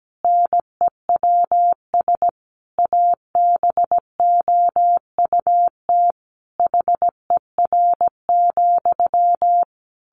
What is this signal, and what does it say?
Morse code, heard as a keyed tone, about 17 wpm, 700 Hz. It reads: NEWS ABOUT HER,
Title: Sentences from Top 1500 Words 17wpm